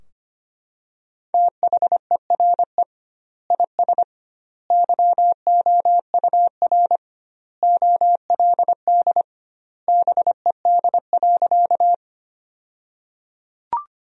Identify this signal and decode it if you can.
Morse code, heard as a keyed tone, about 25 wpm, 700 Hz. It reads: THERE IS YOUR OLD BED.
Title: 2x Word-Spacing - Sentences from Top 500 Words - 25wpm